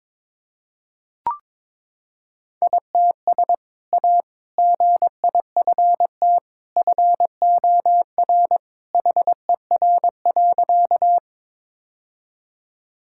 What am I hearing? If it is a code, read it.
Morse code, heard as a keyed tone, about 22 wpm, 700 Hz. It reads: ITS A GIFT FOR HER.
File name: practice / 22wpm / Sentences with 2-Word Phrases - 22wpm